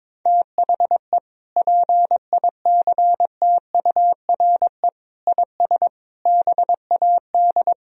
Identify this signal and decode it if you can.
Morse code, heard as a keyed tone, about 21 wpm, 700 Hz. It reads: THE PICTURE IS BAD
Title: Sentences from Top 1000 Words - Review Part 4 - 22wpm